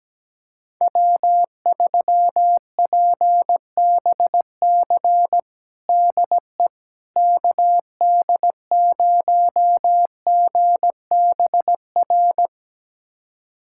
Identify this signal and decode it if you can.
Morse code, heard as a keyed tone, about 17 wpm, 700 Hz. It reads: W3PBC DE KD0GBR
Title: USA Callsign DE Callsign 17wpm